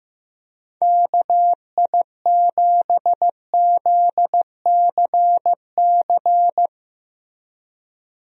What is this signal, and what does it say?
Morse code, heard as a keyed tone, about 15 wpm, 700 Hz. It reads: KI7ZCC
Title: USA Call Signs - 6 Characters - Repeated 2x - 15wpm